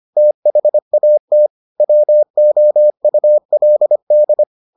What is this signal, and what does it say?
Morse code, heard as a keyed tone, about 25 wpm, 600 Hz. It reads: THAT WOULD